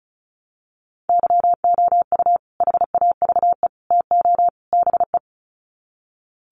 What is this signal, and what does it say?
Morse code, heard as a keyed tone, about 34 wpm, 700 Hz. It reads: YOU HAVE TO BE
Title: Sets of 4 Words 35wpm